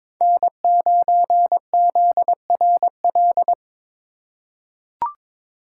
Morse code, heard as a keyed tone, about 22 wpm, 700 Hz. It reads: N9ZRL
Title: USA Call Signs - 3x Character Spacing - 22wpm